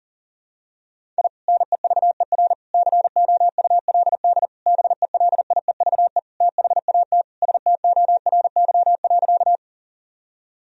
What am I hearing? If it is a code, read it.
Morse code, heard as a keyed tone, about 40 wpm, 700 Hz. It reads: I NEVER COULD BELIEVE THAT STORY.
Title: Sentences from Top 1000 Words - Review Part 1 - 40wpm